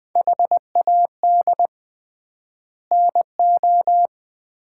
Morse code, heard as a keyed tone, about 20 wpm, 690 Hz. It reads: HAD NO
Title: 3x Word-Spacing - Sets of 5 Words - 20wpm